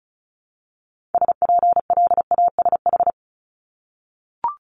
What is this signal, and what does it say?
Morse code, heard as a keyed tone, about 36 wpm, 700 Hz. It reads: SPLASH
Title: Multi-letter Phonemes Repeated 3x 35wpm